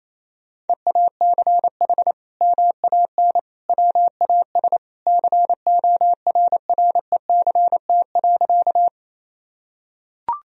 Morse code, heard as a keyed tone, about 28 wpm, 700 Hz. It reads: EACH MAN WAS CORRECT.
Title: Sentences from Top 1000 Words - Review Part 2 - 28wpm